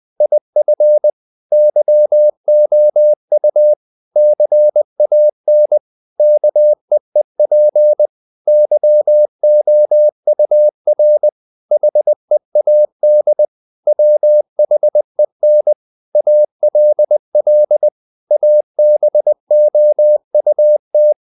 Morse code, heard as a keyed tone, about 20 wpm, 600 Hz. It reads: IF YOU CAN KEEP YOUR HEAD WHEN ALL ABOUT
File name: CW / Practice / If20